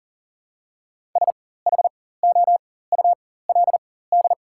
Morse code, heard as a keyed tone, about 40 wpm, spaced as for 19 wpm, 700 Hz. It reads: SHOULD
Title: 4x Character Spacing - Top 500 Words - 40wpm